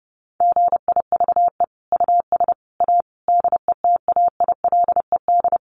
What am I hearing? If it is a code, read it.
Morse code, heard as a keyed tone, about 30 wpm, 700 Hz. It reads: GIVE US A DETAILED
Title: Sentences from Top 1500 Words 30wpm